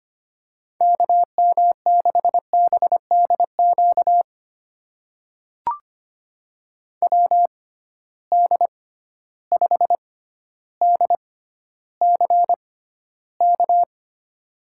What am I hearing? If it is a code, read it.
Morse code, heard as a keyed tone, about 25 wpm, spaced as for 13 wpm, 700 Hz. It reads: KM6BDQ WD5DCK
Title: USA Call Signs - 6 Characters - 6x Character Spacing - 25wpm